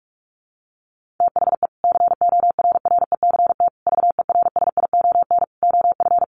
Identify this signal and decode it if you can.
Morse code, heard as a keyed tone, about 44 wpm, 700 Hz. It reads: THE CORRECT VERSION OF
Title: Sentences from Top 1400 Words 45wpm